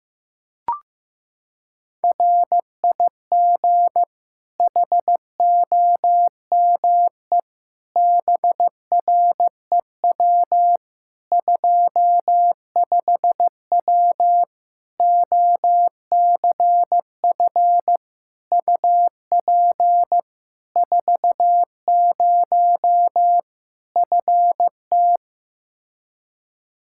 Morse code, heard as a keyed tone, about 15 wpm, 700 Hz. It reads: RIG HOME BREW 25W OCF UP 40 FT
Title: Practice QSO - Exchange 2 - Info - 15wpm